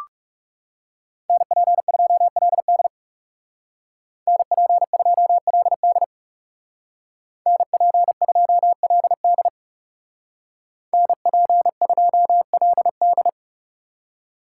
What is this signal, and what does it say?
Morse code, heard as a keyed tone, about 37 wpm, 700 Hz. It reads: NP2LD NP2LD NP2LD NP2LD
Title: Speed Racing - USA Call Signs 30wpm